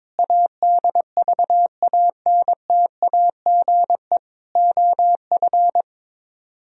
Morse code, heard as a keyed tone, about 23 wpm, 700 Hz. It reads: ADVANTAGE OF
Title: Speed-Racing - Sets of 4 Words 22wpm